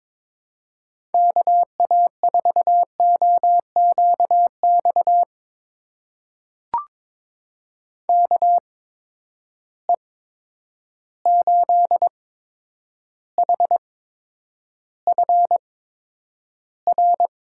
Morse code, heard as a keyed tone, about 22 wpm, spaced as for 9 wpm, 700 Hz. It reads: KA4OQX KE8HFR
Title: USA Call Signs - 6 Characters - 8x Character Spacing - 22wpm